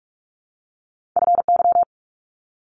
Morse code, heard as a keyed tone, about 46 wpm, 700 Hz. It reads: FY
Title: Multi-letter Phonemes 45wpm